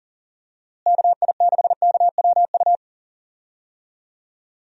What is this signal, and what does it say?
Morse code, heard as a keyed tone, about 40 wpm, 700 Hz. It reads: KI6KWU
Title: USA Call Signs - 6 Characters - Repeated 2x - 40wpm